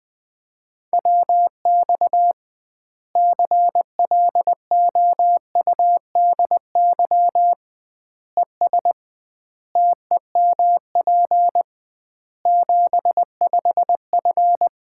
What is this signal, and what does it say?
Morse code, heard as a keyed tone, about 20 wpm, 700 Hz. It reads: WX CLOUDY ES TEMP 75F
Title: Common QSO Phrases - 2x Word-Spacing - 20wpm